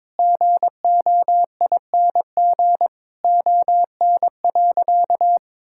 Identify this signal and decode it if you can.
Morse code, heard as a keyed tone, about 22 wpm, 700 Hz. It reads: GOING ON.
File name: Sentences with 7-Word Phrases - 22wpm